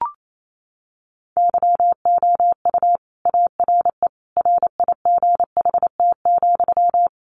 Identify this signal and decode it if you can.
Morse code, heard as a keyed tone, about 28 wpm, 700 Hz. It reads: YOU ARE RIGHT,
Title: Sentences from Top 300 Words 28wpm